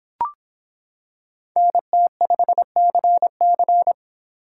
Morse code, heard as a keyed tone, about 25 wpm, 700 Hz. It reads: NT5CC